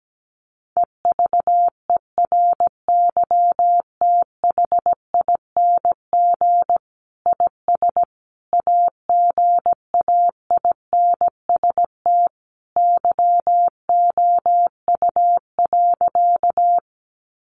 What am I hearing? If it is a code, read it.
Morse code, heard as a keyed tone, about 17 wpm, 700 Hz. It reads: EVERYTHING IS AGAINST YOU.
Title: Sentences from Top 700 Words 17wpm